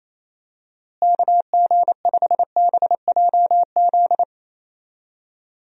Morse code, heard as a keyed tone, about 28 wpm, 700 Hz. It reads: KG5BJZ
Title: USA Call Signs - 6 Characters - Repeated 2x - 28wpm